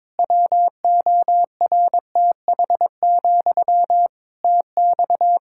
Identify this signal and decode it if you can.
Morse code, heard as a keyed tone, about 22 wpm, 700 Hz. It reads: WORTH, TX
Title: Practice QSO - Exchange 1 - Beginning - 22wpm